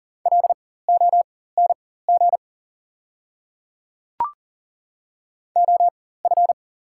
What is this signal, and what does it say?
Morse code, heard as a keyed tone, about 40 wpm, spaced as for 19 wpm, 700 Hz. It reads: LONG OF
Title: 4x Character Spacing - Top 100 Words - 40wpm